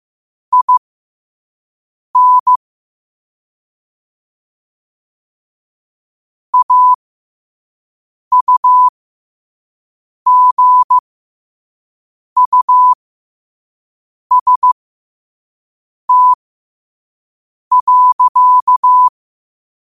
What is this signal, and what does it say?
Morse code, heard as a keyed tone, about 15 wpm, spaced as for 5 wpm, 1000 Hz. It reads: IN AUGUST.